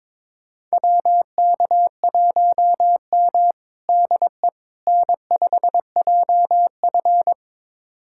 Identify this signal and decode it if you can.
Morse code, heard as a keyed tone, about 22 wpm, 700 Hz. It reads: WK1M DE N5JF
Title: USA Callsign DE Callsign - 4 Characters - 22wpm